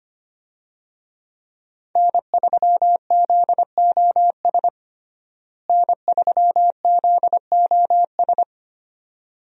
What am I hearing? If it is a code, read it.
Morse code, heard as a keyed tone, about 25 wpm, 700 Hz. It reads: N3ZOS N3ZOS
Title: USA Call Signs - 5 Characters - Repeated 2x - 25wpm